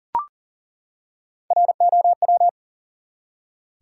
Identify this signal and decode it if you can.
Morse code, heard as a keyed tone, about 40 wpm, 700 Hz. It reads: ROW
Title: Three Letter Words - 40wpm